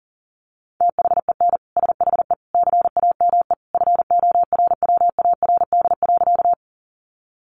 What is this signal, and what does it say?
Morse code, heard as a keyed tone, about 40 wpm, 700 Hz. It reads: THEN SHE CAME FORWARD.
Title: Sentences from Top 1300 Words 40wpm